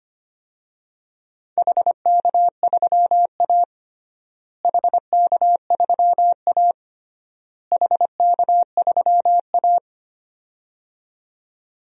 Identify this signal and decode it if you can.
Morse code, heard as a keyed tone, about 25 wpm, 700 Hz. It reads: HK3A HK3A HK3A
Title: DXCC-100 Call Signs - 4 Characters - Repeated 3x - 25wpm